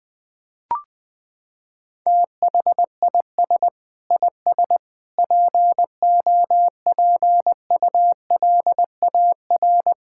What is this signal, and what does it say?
Morse code, heard as a keyed tone, about 19 wpm, 700 Hz. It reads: THIS IS POPULAR
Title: Sentences with 5-Word Phrases - 20wpm